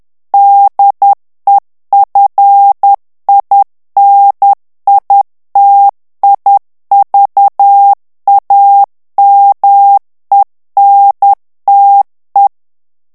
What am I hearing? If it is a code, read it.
Morse code, heard as a keyed tone, about 10 wpm, 800 Hz. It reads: DEFINITIVAMENTE
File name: ITALIEN texte clair 1 vitesse 10 mots minute